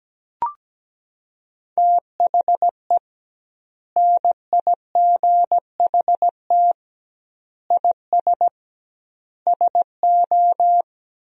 Morse code, heard as a keyed tone, about 17 wpm, 700 Hz. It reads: THE NIGHT IS SO
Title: 2x Word-Spacing - Sentences from Top 500 Words - 17wpm